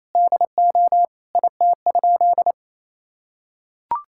Morse code, heard as a keyed tone, about 28 wpm, 700 Hz. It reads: DO IT?
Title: Sentences from Top 1000 Words - Review Part 5 - 28wpm